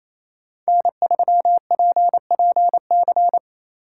Morse code, heard as a keyed tone, about 28 wpm, 700 Hz. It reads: N3PPC
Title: USA Call Signs - 4x Character Spacing - 28wpm